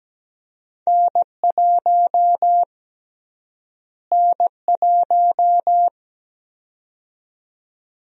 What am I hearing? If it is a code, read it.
Morse code, heard as a keyed tone, about 17 wpm, 700 Hz. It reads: N1 N1
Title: USA Call Sign Prefixes - Repeated 2x - 17wpm